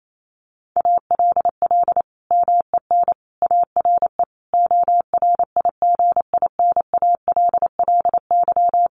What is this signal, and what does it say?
Morse code, heard as a keyed tone, about 28 wpm, 700 Hz. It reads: ALL MEN ARE ORIGINALLY
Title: Sentences from Top 1500 Words 28wpm